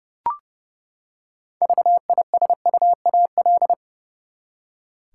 Morse code, heard as a keyed tone, about 29 wpm, 700 Hz. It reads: VISUAL
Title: Six Letter Words - 30wpm